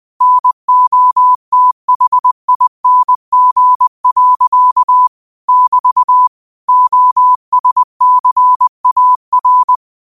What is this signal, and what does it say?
Morse code, heard as a keyed tone, about 20 wpm, 1000 Hz. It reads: NOTHING. <BT> OSCAR